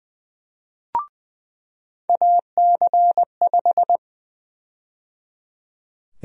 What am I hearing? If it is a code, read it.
Morse code, heard as a keyed tone, about 20 wpm, 700 Hz. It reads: AC5
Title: USA Call Sign Prefixes - 20wpm